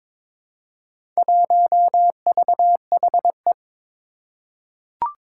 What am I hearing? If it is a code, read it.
Morse code, heard as a keyed tone, about 21 wpm, 700 Hz. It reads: 1VHE
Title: Four Challenging Letter-Number Combinations 22wpm